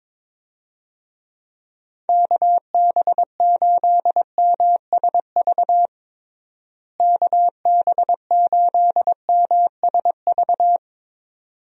Morse code, heard as a keyed tone, about 22 wpm, 700 Hz. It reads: KB8MSV KB8MSV
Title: USA Call Signs - 6 Characters - Repeated 2x - 22wpm